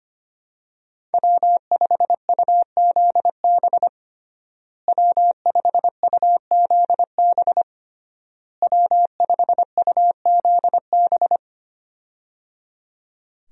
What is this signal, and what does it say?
Morse code, heard as a keyed tone, about 25 wpm, 700 Hz. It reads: W5UZB W5UZB W5UZB
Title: USA Call Signs - 5 Characters - Repeated 3x - 25wpm